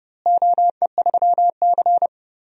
Morse code, heard as a keyed tone, about 30 wpm, 700 Hz. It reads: OE3C
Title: DXCC-100 Call Signs - 4 Characters - Repeated 2x - 30wpm